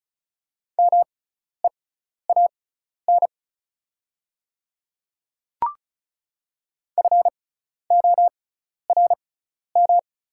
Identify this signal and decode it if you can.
Morse code, heard as a keyed tone, about 36 wpm, spaced as for 12 wpm, 700 Hz. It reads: MEAN FORM
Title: 6x Character Spacing - Top 300 Words - 35wpm